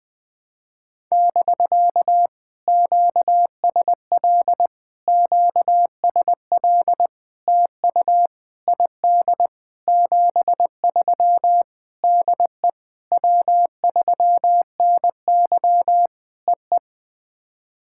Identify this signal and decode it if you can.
Morse code, heard as a keyed tone, about 20 wpm, 700 Hz. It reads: <BK> QSL QSL TU ID 73 DE W3NY EE
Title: POTA QSO Practice - 20wpm